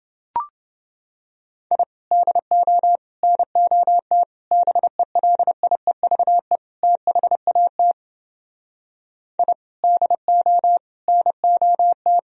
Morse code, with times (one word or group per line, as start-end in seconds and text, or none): 1.7-1.8 I
2.1-3.0 DO
3.2-4.2 NOT
4.5-6.6 BELIEVE
6.8-7.9 THAT
9.4-9.5 I
9.8-10.8 DO
11.1-12.2 NOT